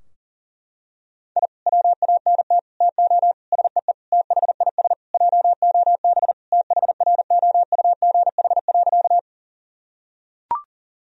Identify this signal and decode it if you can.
Morse code, heard as a keyed tone, about 40 wpm, 700 Hz. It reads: I WANT TO SEE THIS JOB THROUGH.
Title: Sentences from Top 1000 Words - Review Part 3 - 40wpm